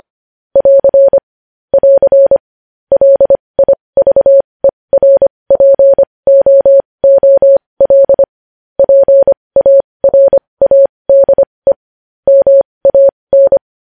Morse code, heard as a keyed tone, about 25 wpm, spaced as for 20 wpm, 550 Hz. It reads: <AR> <AR> LIVERPOOL PARADE MAN
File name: podcast / News headlines-1x-25x20